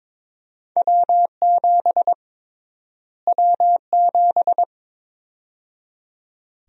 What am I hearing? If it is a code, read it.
Morse code, heard as a keyed tone, about 22 wpm, 700 Hz. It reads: W7 W7
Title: USA Call Sign Prefixes - Repeated 2x - 22wpm